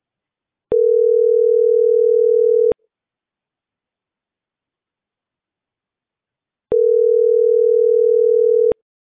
Ringback tone (North America), 440 + 480 Hz.